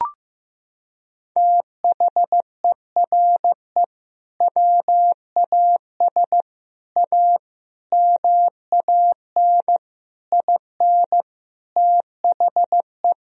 Morse code, heard as a keyed tone, about 15 wpm, 700 Hz. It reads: THERE WAS A MAN IN THE
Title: Sentences from Top 1000 Words - Review Part 3 - 15wpm